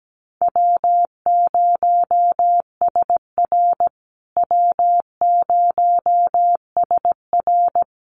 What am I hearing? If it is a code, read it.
Morse code, heard as a keyed tone, about 17 wpm, 700 Hz. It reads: W0SR W0SR
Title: USA Call Signs Repeated 3x 17wpm